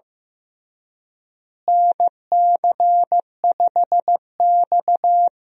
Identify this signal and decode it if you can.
Morse code, heard as a keyed tone, about 15 wpm, 700 Hz. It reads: NC5X
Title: USA Call Signs - 4 Characters - Repeated 3x - 15wpm